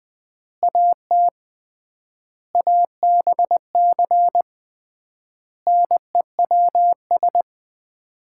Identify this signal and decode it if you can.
Morse code, heard as a keyed tone, about 20 wpm, 700 Hz. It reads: AT ABC NEWS
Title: 3x Word-Spacing - Sets of 5 Words - 20wpm